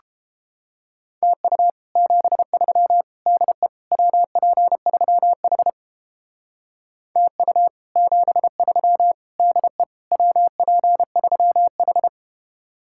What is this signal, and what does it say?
Morse code, heard as a keyed tone, about 31 wpm, 700 Hz. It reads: TU 73 DE WP3H TU 73 DE WP3H